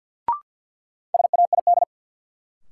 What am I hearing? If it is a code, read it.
Morse code, heard as a keyed tone, about 53 wpm, 700 Hz. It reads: SAID
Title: Sentences from Top 100 Words - New Word Drill - 50wpm